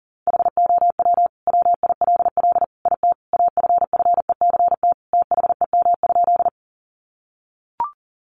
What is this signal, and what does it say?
Morse code, heard as a keyed tone, about 40 wpm, 700 Hz. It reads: HOW WILL IT AFFECT THEM?
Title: Sentences from Top 900 Words 40wpm